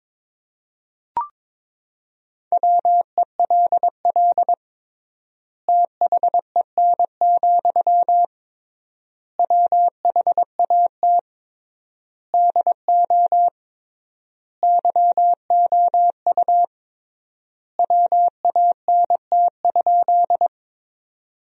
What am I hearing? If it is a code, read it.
Morse code, heard as a keyed tone, about 22 wpm, 700 Hz. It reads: WELL THEN, WHAT DO YOU WANT?
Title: 3x Word-Spacing - Sentences from Top 200 Words - 22wpm